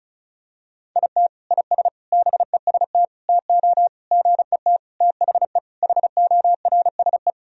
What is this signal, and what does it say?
Morse code, heard as a keyed tone, about 34 wpm, 700 Hz. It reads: IT IS BEST TO GET THE HORSE
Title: Sentences from Top 1000 Words - Review Part 1 - 35wpm